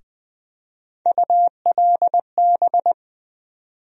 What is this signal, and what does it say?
Morse code, heard as a keyed tone, about 21 wpm, 700 Hz. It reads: ULB
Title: USA Call Sign Postfixes - 3 Letters 20wpm